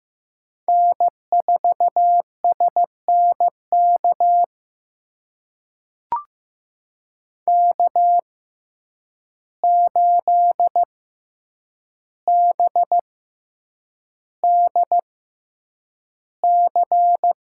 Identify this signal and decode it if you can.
Morse code, heard as a keyed tone, about 15 wpm, spaced as for 8 wpm, 700 Hz. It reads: N4SNK K8BDC
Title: USA Call Signs - 5 Characters - 6x Character Spacing - 15wpm